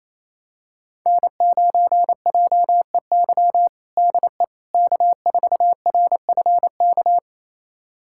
Morse code, heard as a keyed tone, about 28 wpm, 700 Hz. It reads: N9JEY DE K4RFK
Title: USA Callsign DE Callsign - 5 Characters - 28wpm